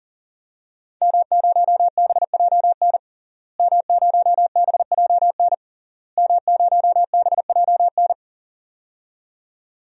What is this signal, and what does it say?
Morse code, heard as a keyed tone, about 40 wpm, 700 Hz. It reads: M0BJN M0BJN M0BJN
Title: DXCC-100 Call Signs - 5 Characters - Repeated 3x - 40wpm